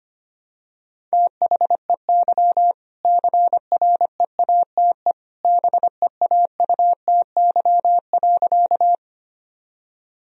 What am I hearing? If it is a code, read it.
Morse code, heard as a keyed tone, about 25 wpm, 700 Hz. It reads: THEY CREATE BEAUTY.
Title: Sentences from Top 1000 Words - Review Part 3 - 25wpm